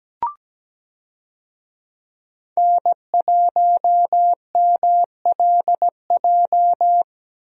Morse code, heard as a keyed tone, about 17 wpm, 700 Hz. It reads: N1MLJ
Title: USA Call Signs - 5 Characters - Repeated 3x - 17wpm